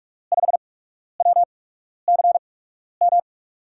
Morse code, heard as a keyed tone, about 42 wpm, spaced as for 12 wpm, 700 Hz. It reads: 5WCM
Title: USA Call Signs - 6 Characters - 8x Character Spacing - 45wpm